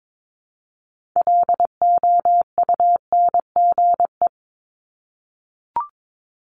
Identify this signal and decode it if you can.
Morse code, heard as a keyed tone, about 22 wpm, 700 Hz. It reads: LOUNGE
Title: Multi-letter Phonemes Repeated 3x 22wpm